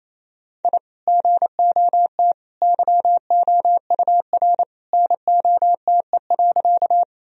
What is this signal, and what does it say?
Morse code, heard as a keyed tone, about 28 wpm, 700 Hz. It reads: I GOT YOUR NOTE.
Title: Sentences from Top 1000 Words - Review Part 2 - 28wpm